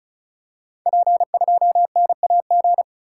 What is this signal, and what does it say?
Morse code, heard as a keyed tone, about 36 wpm, 700 Hz. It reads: P2NAG
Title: Code Groups - 5 Letter:Numbers - 35wpm